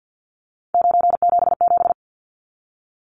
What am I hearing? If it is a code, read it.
Morse code, heard as a keyed tone, about 50 wpm, 700 Hz. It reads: 977
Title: Numbers - 3-Digit 50wpm